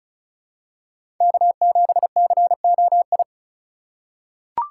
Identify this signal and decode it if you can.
Morse code, heard as a keyed tone, about 34 wpm, 700 Hz. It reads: K7COI
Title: USA Call Signs - 5 Characters - 2x Character Spacing - 35wpm